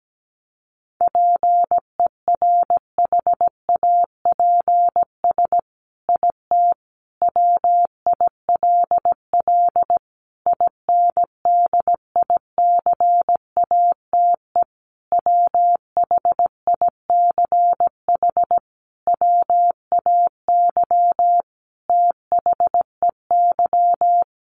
Morse code, heard as a keyed tone, about 17 wpm, 700 Hz. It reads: PERHAPS IT WILL INDICATE WHICH WAY THEY
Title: Sentences from Top 1300 Words 17wpm